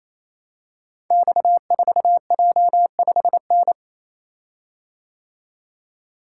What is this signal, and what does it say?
Morse code, heard as a keyed tone, about 27 wpm, 700 Hz. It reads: X4J5N
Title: Code Groups - 5 Letter:Numbers - 28wpm